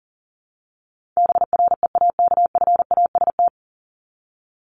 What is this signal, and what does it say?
Morse code, heard as a keyed tone, about 40 wpm, 700 Hz. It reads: BREAKFAST
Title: Top 1251-1500 Words 40wpm